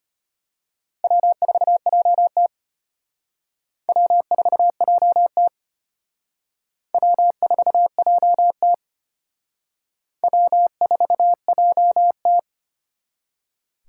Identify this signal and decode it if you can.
Morse code, heard as a keyed tone, about 32 wpm, 700 Hz. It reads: W4JT W4JT W4JT W4JT